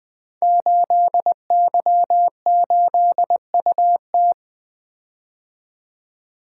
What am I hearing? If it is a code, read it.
Morse code, heard as a keyed tone, about 20 wpm, 700 Hz. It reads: 8Y8UT
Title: Code Groups - 5 Letter:Numbers - 20wpm